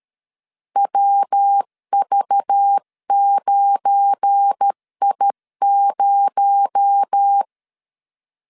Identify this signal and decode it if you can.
Morse code, heard as a keyed tone, about 13 wpm, 800 Hz. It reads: WV9I0
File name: LEZIONI / 9 286 12wpm 2305